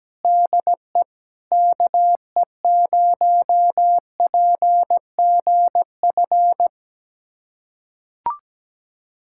Morse code, heard as a keyed tone, about 17 wpm, 700 Hz. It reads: DE KE0PGF
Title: USA Callsign DE Callsign - 6 Characters - 17wpm